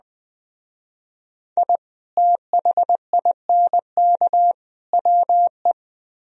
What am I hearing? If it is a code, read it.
Morse code, heard as a keyed tone, about 20 wpm, 700 Hz. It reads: I THINK WE
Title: Speed-Racing - Sets of 3 Words 17wpm